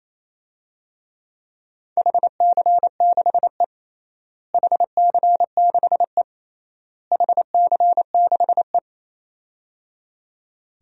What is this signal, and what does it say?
Morse code, heard as a keyed tone, about 28 wpm, 700 Hz. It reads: HC6E HC6E HC6E
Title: DXCC-100 Call Signs - 4 Characters - Repeated 3x - 28wpm